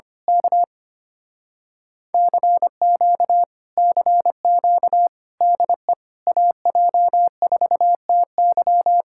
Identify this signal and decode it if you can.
Morse code, heard as a keyed tone, about 24 wpm, 700 Hz. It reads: K CQ CQ DE AJ4TY
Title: Speed-Racing Common QSO Phrases - 25wpm